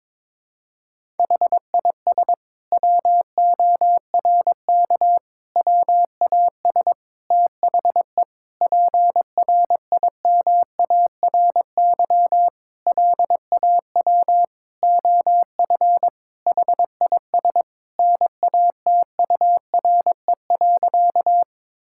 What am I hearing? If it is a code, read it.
Morse code, heard as a keyed tone, about 22 wpm, 700 Hz. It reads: HIS WORK WAS THE PRIMARY LAW OF HIS NATURE.
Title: Sentences from Top 1000 Words - Review Part 5 - 22wpm